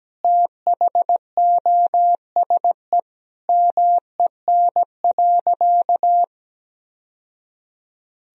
Morse code, heard as a keyed tone, about 17 wpm, 700 Hz. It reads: THOSE MEN.
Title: Sentences from Top 1000 Words - Review Part 1 - 17wpm